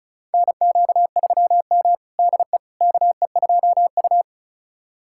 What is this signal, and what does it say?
Morse code, heard as a keyed tone, about 34 wpm, 700 Hz. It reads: NQ3M DE KE2U
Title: USA Callsign DE Callsign - 4 Characters - 35wpm